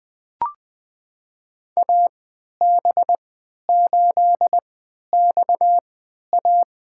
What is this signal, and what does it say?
Morse code, heard as a keyed tone, about 20 wpm, spaced as for 11 wpm, 700 Hz. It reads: AB8XA
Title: USA Call Signs - 5 Characters - 3x Character Spacing - 20wpm